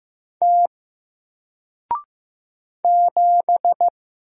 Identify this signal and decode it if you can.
Morse code, heard as a keyed tone, about 15 wpm, 700 Hz. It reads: T 7
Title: Single Letter-Number Alt 15wpm